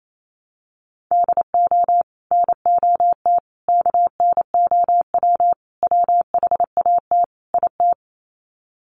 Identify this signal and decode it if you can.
Morse code, heard as a keyed tone, about 28 wpm, 700 Hz. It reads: DO NOT KNOW WHAT IT
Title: Sets of 5 Words 28wpm